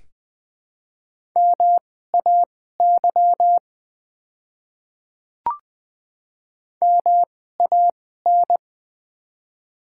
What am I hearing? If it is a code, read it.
Morse code, heard as a keyed tone, about 21 wpm, spaced as for 15 wpm, 700 Hz. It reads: MAY MAN
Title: 2x Character Spacing - Top 200 Words - 20wpm